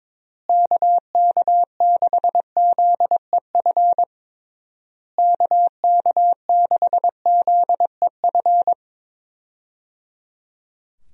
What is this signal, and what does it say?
Morse code, heard as a keyed tone, about 21 wpm, 700 Hz. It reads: KK6ZEF KK6ZEF
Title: USA Call Signs - 6 Characters - Repeated 3x - 22wpm